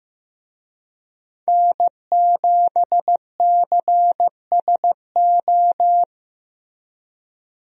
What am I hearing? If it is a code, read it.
Morse code, heard as a keyed tone, about 15 wpm, 700 Hz. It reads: N7CSO
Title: USA Call Signs - 5 Characters - Repeated 3x - 15wpm